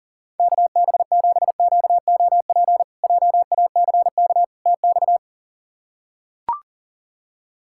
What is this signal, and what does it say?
Morse code, heard as a keyed tone, about 40 wpm, 700 Hz. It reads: KB7QOP JACK TX
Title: K1USN SST QSO Practice - 40wpm